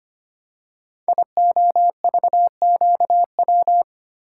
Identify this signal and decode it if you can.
Morse code, heard as a keyed tone, about 25 wpm, 700 Hz. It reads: IOVQW
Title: Code Groups - 5 Letter:Numbers - 25wpm